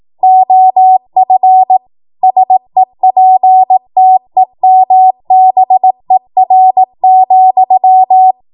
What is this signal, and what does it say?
Morse code, heard as a keyed tone, about 18 wpm, 750 Hz. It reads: OF SEPTEMBER,
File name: Morse / 18WPM